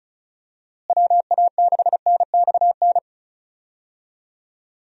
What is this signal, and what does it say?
Morse code, heard as a keyed tone, about 36 wpm, 700 Hz. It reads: WA6NXN
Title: USA Call Signs - 6 Characters - Repeated 2x - 35wpm